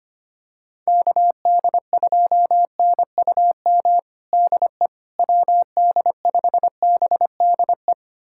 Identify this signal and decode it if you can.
Morse code, heard as a keyed tone, about 25 wpm, 700 Hz. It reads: KD2NUM DE WD5BDE
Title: USA Callsign DE Callsign 25wpm